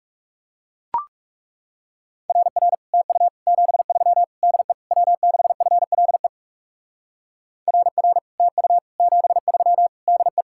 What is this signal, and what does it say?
Morse code, heard as a keyed tone, about 39 wpm, 700 Hz. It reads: RR TU 73 DE W6FLE RR TU 73 DE